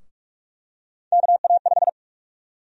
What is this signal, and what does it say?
Morse code, heard as a keyed tone, about 42 wpm, 700 Hz. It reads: KA5
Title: USA Call Sign Prefixes - 45wpm